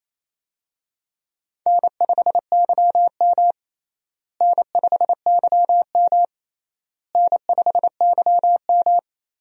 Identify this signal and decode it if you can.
Morse code, heard as a keyed tone, about 28 wpm, 700 Hz. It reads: N5YM N5YM N5YM
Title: USA Call Signs - 4 Characters - Repeated 3x - 28wpm